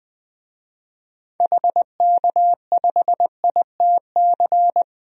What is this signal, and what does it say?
Morse code, heard as a keyed tone, about 21 wpm, 700 Hz. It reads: HK5ITC
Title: DXCC-100 Call Signs - 6 Characters - Repeated 2x - 20wpm